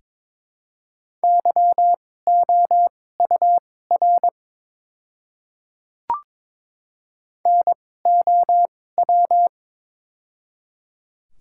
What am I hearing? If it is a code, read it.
Morse code, heard as a keyed tone, about 22 wpm, spaced as for 16 wpm, 700 Hz. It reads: YOUR NOW
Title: 2x Character Spacing - Top 300 Words - 22wpm